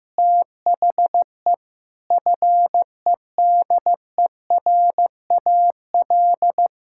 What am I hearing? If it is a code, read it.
Morse code, heard as a keyed tone, about 15 wpm, 700 Hz. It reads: THE FEDERAL
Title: Sentences from Top 1000 Words - Review Part 5 - 15wpm